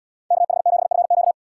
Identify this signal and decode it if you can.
Morse code, heard as a keyed tone, about 77 wpm, 690 Hz. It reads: KH7FQ